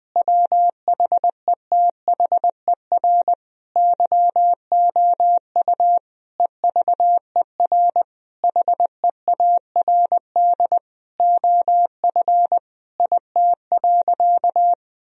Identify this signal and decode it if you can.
Morse code, heard as a keyed tone, about 20 wpm, 700 Hz. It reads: WHETHER YOU EVER HEARD OF IT.
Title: Sentences from Top 1000 Words - Review Part 3 - 20wpm